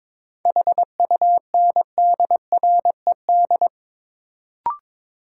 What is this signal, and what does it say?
Morse code, heard as a keyed tone, about 22 wpm, 700 Hz. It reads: HUNDRED
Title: Sentences from Top 400 Words - New Word Drill - 22wpm